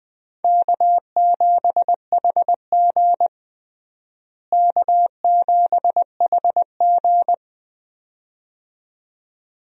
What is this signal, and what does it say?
Morse code, heard as a keyed tone, about 20 wpm, 700 Hz. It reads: K7HG K7HG
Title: USA Call Signs - 4 Characters - Repeated 2x - 20wpm